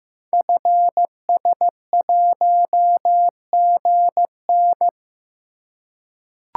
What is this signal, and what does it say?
Morse code, heard as a keyed tone, about 15 wpm, 700 Hz. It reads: FS1GN
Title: Code Groups - 5 Letter:Numbers - 15wpm